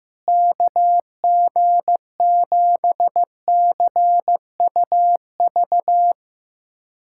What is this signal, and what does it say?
Morse code, heard as a keyed tone, about 15 wpm, 700 Hz. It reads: KG7CUV
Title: USA Callsign DE Callsign - 6 Characters - 15wpm